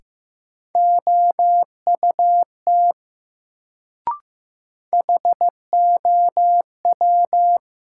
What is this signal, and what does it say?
Morse code, heard as a keyed tone, about 15 wpm, 700 Hz. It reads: OUT HOW
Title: Sets of 2 Words - New Word Drill - 15wpm